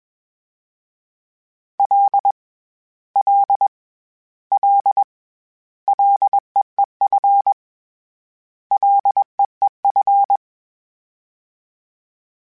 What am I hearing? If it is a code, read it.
Morse code, heard as a keyed tone, about 21 wpm, 800 Hz. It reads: L L L LEEF LEEF
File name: Short Runs / Short Runs-mp3s / SinglePrefixes @20